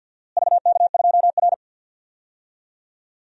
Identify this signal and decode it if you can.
Morse code, heard as a keyed tone, about 47 wpm, 700 Hz. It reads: VK2R
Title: DXCC-100 Call Signs - 4 Characters - Repeated 3x - 50wpm